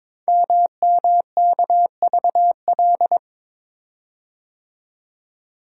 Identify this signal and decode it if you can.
Morse code, heard as a keyed tone, about 22 wpm, 700 Hz. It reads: MMKVL
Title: Code Groups - 5 Letters - 22wpm